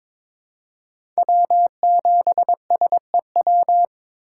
Morse code, heard as a keyed tone, about 22 wpm, 700 Hz. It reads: W7SEW
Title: USA Call Signs - 5 Characters - 2x Character Spacing - 22wpm